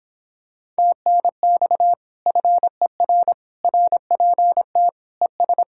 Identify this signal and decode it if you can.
Morse code, heard as a keyed tone, about 26 wpm, 700 Hz. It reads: TNX FER RPT ES